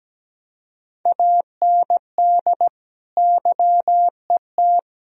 Morse code, heard as a keyed tone, about 17 wpm, 700 Hz. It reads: AND YET
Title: Sentences from Top 1000 Words - Review Part 4 - 17wpm